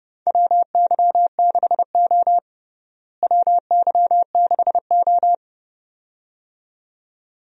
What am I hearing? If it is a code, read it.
Morse code, heard as a keyed tone, about 30 wpm, 700 Hz. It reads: WY6O WY6O
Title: USA Call Signs - 4 Characters - Repeated 2x - 30wpm